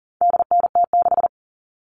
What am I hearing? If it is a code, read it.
Morse code, heard as a keyed tone, about 40 wpm, 700 Hz. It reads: DNT6
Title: All Cut Numbers - 4-Digit 40wpm